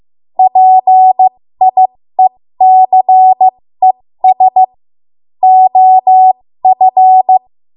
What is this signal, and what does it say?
Morse code, heard as a keyed tone, about 15 wpm, spaced as for 13 wpm, 750 Hz. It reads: PIECES OF
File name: Morse / 13WPM